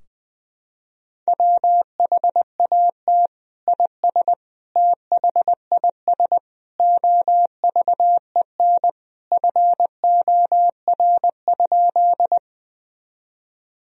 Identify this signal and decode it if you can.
Morse code, heard as a keyed tone, about 20 wpm, 700 Hz. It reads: WHAT IS THIS OVEN FOR?
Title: Sentences from Top 1000 Words - Review Part 4 - 20wpm